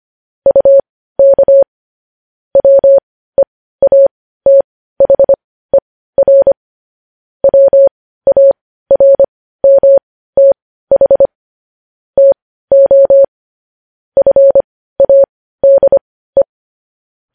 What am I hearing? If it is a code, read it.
Morse code, heard as a keyed tone, about 25 wpm, spaced as for 15 wpm, 550 Hz. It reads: UK WEATHER WARMTH TO FADE
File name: podcast / News headlines-1x-25x15